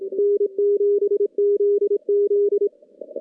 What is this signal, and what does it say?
Morse code, heard as a keyed tone, about 25 wpm, 410 Hz. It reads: N7ZZ
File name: contests / CW-SS-k5tr- -2300